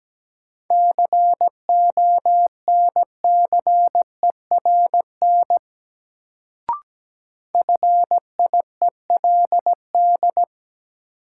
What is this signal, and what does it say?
Morse code, heard as a keyed tone, about 17 wpm, 700 Hz. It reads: CONCERN FIELD
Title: Sentences from Top 1000 Words - New Word Drill - 17wpm